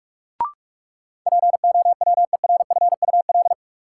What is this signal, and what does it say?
Morse code, heard as a keyed tone, about 43 wpm, 700 Hz. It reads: POWERFUL
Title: Sentences from Top 1300 Words - New Word Drill Alt - 45wpm